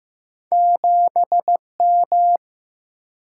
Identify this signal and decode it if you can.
Morse code, heard as a keyed tone, about 15 wpm, 700 Hz. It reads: 7M
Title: Single-Letter Single-Number 15wpm